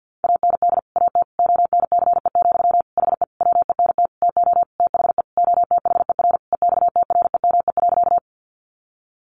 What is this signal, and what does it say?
Morse code, heard as a keyed tone, about 50 wpm, 700 Hz. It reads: AND AT ONCE, HE WENT TO THE OTHER EXTREME.
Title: Sentences from Top 1300 Words 50wpm